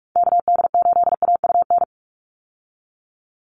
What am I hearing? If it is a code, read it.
Morse code, heard as a keyed tone, about 44 wpm, 700 Hz. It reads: KD8AUN
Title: USA Call Signs 45wpm